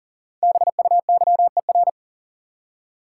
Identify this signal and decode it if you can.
Morse code, heard as a keyed tone, about 40 wpm, 700 Hz. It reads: BUYER
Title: Five Letter Words - 40wpm